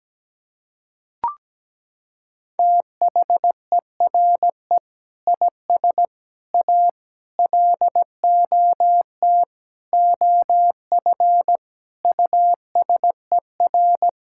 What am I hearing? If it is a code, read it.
Morse code, heard as a keyed tone, about 17 wpm, 700 Hz. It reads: THERE IS A LOT OF USER
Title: Sentences from Top 1000 Words - Review Part 6 - 17wpm